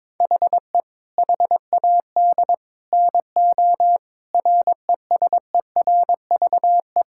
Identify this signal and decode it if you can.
Morse code, heard as a keyed tone, about 22 wpm, 700 Hz. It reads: HE HAD NO RESERVE
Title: Sentences from Top 1000 Words - Review Part 6 - 22wpm